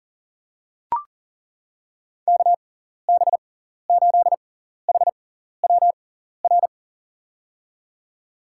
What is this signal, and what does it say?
Morse code, heard as a keyed tone, about 40 wpm, spaced as for 14 wpm, 700 Hz. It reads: KB8HWR
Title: USA Call Signs - 6 Characters - 6x Character Spacing - 40wpm